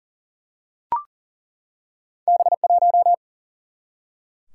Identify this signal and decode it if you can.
Morse code, heard as a keyed tone, about 42 wpm, 700 Hz. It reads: B 1
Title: Single-Letter Single-Number 40wpm